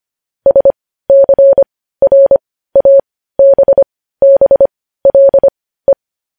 Morse code, heard as a keyed tone, about 25 wpm, spaced as for 15 wpm, 550 Hz. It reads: SCRABBLE
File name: podcast / News headlines-1x-25x15